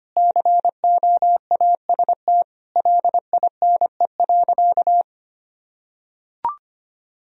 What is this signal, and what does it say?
Morse code, heard as a keyed tone, about 25 wpm, 700 Hz. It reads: COAST LINE.
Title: Sentences from Top 1000 Words - Review Part 4 - 25wpm